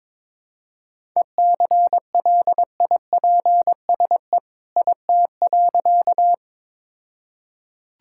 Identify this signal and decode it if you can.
Morse code, heard as a keyed tone, about 22 wpm, 690 Hz. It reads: ECLIPSE IT.
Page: morse-code-ninja-practice-22wpm